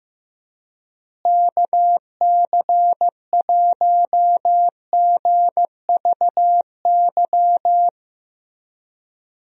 Morse code, heard as a keyed tone, about 15 wpm, 700 Hz. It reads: KC1GVY
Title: USA Call Signs - 6 Characters - Repeated 2x - 15wpm